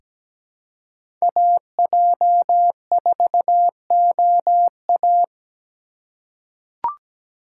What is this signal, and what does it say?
Morse code, heard as a keyed tone, about 17 wpm, 700 Hz. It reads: AJ4OA
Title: USA Call Signs - 5 Characters - 3x Character Spacing - 17wpm